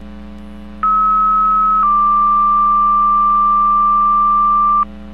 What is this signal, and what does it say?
Two-tone page, heard as a one-second tone, then a three-second tone, about 1290 then 1180 Hz. It is heard from 0.8 to 4.8 s.